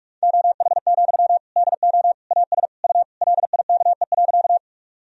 Morse code, heard as a keyed tone, about 46 wpm, 690 Hz. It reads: OH, DO AS U LIKE.
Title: Sentences from Top 500 Words and Abbreviations 45wpm